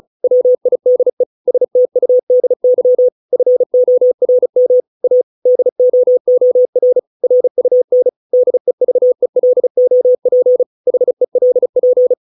Morse code, heard as a keyed tone, about 34 wpm, 500 Hz. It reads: WIDE STUDY FORM A DOOR RUN DEVELOP HELP